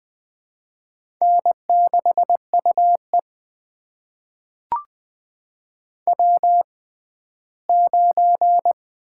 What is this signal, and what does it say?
Morse code, heard as a keyed tone, about 20 wpm, spaced as for 14 wpm, 700 Hz. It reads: N6UE W9
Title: USA Call Signs - 6x Character Spacing - 20wpm